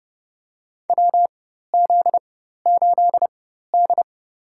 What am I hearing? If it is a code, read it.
Morse code, heard as a keyed tone, about 30 wpm, spaced as for 14 wpm, 700 Hz. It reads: WZ8D
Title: USA Call Signs - 4x Character Spacing - 30wpm